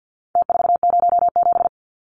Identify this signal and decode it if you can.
Morse code, heard as a keyed tone, about 50 wpm, 700 Hz. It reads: T407
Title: Numbers - Contest Style - 4-Digit 50wpm